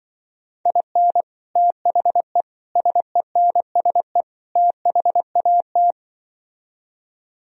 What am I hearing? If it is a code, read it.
Morse code, heard as a keyed tone, about 24 wpm, 700 Hz. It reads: IN THE SENSE THAT
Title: Speed-Racing - Sets of 4 Words 20wpm